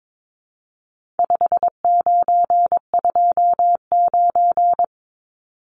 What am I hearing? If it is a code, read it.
Morse code, heard as a keyed tone, about 22 wpm, 700 Hz. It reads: 5929
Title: Numbers - 4-Digit 22wpm